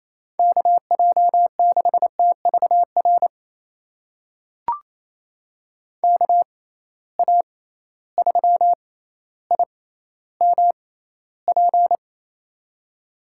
Morse code, heard as a keyed tone, about 28 wpm, spaced as for 14 wpm, 700 Hz. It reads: KJ6TVR KA3IMP
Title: USA Call Signs - 6 Characters - 6x Character Spacing - 28wpm